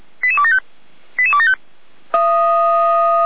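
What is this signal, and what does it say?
Five-tone selective call (ZVEI), heard as quick runs of five tones, tones of about 70 ms. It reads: 88275 88275
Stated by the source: ZVEI five-tone selective call received via SDR.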